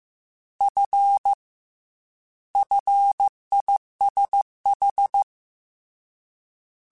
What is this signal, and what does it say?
Morse code, heard as a keyed tone, about 15 wpm, 800 Hz. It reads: F FISH
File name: F and L / F and L mp3 / F and L Prefixes 3x2 Twice @13